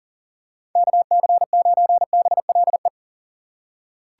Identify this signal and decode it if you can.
Morse code, heard as a keyed tone, about 38 wpm, 700 Hz. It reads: KC9BLE
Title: USA Call Signs - 4x Character Spacing - 40wpm